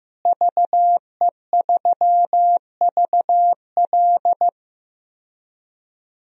Morse code, heard as a keyed tone, about 15 wpm, 700 Hz. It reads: VE3VL
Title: DXCC-100 Call Signs - 5 Characters - Repeated 2x - 15wpm